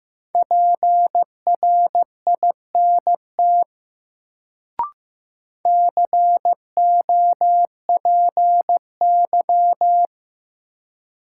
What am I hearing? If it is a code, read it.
Morse code, heard as a keyed tone, about 15 wpm, 700 Hz. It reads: PRINT COPY
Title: Sentences from Top 1400 Words - New Word Drill Alt - 15wpm